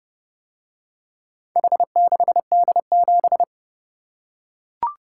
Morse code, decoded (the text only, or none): H6D7